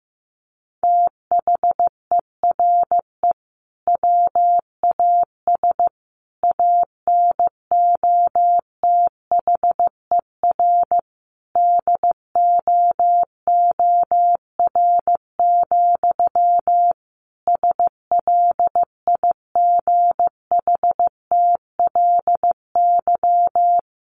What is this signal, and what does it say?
Morse code, heard as a keyed tone, about 15 wpm, 700 Hz. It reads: THERE WAS ANOTHER DOOR, SLIGHTLY
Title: Sentences from Top 1300 Words 15wpm